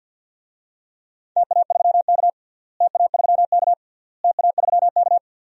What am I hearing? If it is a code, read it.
Morse code, heard as a keyed tone, about 53 wpm, 700 Hz. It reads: TA3K TA3K TA3K
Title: DXCC-100 Call Signs - 4 Characters - Repeated 3x - 50wpm